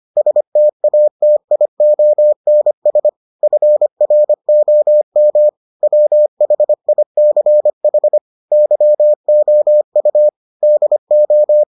Morse code, heard as a keyed tone, about 25 wpm, 600 Hz. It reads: STATIONS FROM WHICH YOU DO